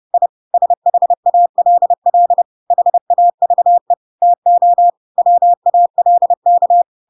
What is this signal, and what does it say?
Morse code, heard as a keyed tone, about 30 wpm, 700 Hz. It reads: I SHALL HAVE TO WALK
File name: Pinocchio / Pinocchio ch7 30 30 ik1hge